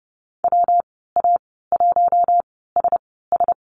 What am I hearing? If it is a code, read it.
Morse code, heard as a keyed tone, about 30 wpm, spaced as for 17 wpm, 700 Hz. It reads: WA1SS
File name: Lesson 19g - Letter w - Call Signs 30wpm